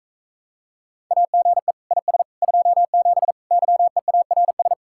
Morse code, heard as a keyed tone, about 44 wpm, 700 Hz. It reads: AGE IS 27 YEARS